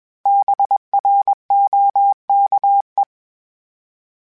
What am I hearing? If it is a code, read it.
Morse code, heard as a keyed tone, about 21 wpm, 800 Hz. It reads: BROKE